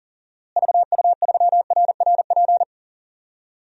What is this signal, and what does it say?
Morse code, heard as a keyed tone, about 40 wpm, 700 Hz. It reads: VU3RRP
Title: DXCC-100 Call Signs - 6 Characters - Repeated 2x - 40wpm